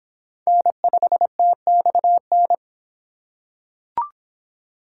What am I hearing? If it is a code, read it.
Morse code, heard as a keyed tone, about 25 wpm, 700 Hz. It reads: N5TXN